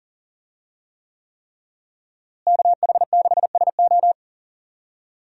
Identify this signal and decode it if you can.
Morse code, heard as a keyed tone, about 40 wpm, 700 Hz. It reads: KH6SO
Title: USA Call Signs - 5 Characters - Repeated 3x - 40wpm